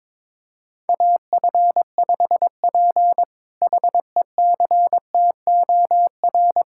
Morse code, heard as a keyed tone, about 22 wpm, 700 Hz. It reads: AF5P HECTOR
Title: K1USN SST QSO Practice - 22wpm